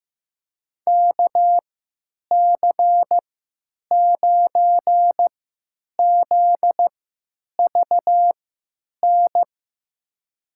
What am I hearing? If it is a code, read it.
Morse code, heard as a keyed tone, about 15 wpm, spaced as for 9 wpm, 700 Hz. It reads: KC9ZVN
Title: USA Call Signs - 6 Characters - 3x Character Spacing - 15wpm